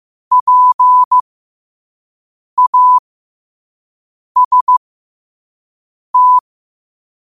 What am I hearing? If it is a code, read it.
Morse code, heard as a keyed tone, about 15 wpm, spaced as for 5 wpm, 1000 Hz. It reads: PAST
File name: qotdfiles / 5wpm